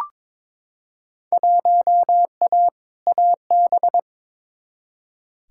Morse code, heard as a keyed tone, about 23 wpm, 700 Hz. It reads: 1A AB
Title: Field Day Exchanges - 22wpm